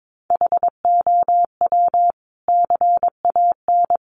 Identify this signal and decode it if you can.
Morse code, heard as a keyed tone, about 22 wpm, 700 Hz. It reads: HOW CAN
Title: Sentences from Top 200 Words 22wpm